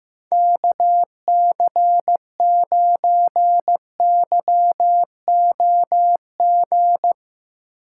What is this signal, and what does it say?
Morse code, heard as a keyed tone, about 15 wpm, 700 Hz. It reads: KC9YOG
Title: USA Call Signs - 6 Characters - 4x Character Spacing - 15wpm